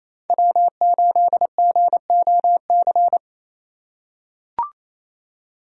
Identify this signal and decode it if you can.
Morse code, heard as a keyed tone, about 28 wpm, 700 Hz. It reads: W8GOC
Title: USA Call Signs - 5 Characters - 2x Character Spacing - 28wpm